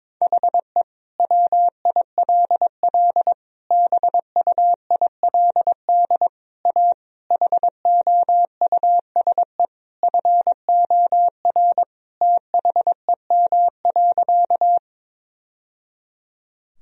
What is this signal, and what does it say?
Morse code, heard as a keyed tone, about 22 wpm, 700 Hz. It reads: HE WILL BUILD A HOUSE FOR THEM.
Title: Sentences from Top 1000 Words - Review Part 1 - 22wpm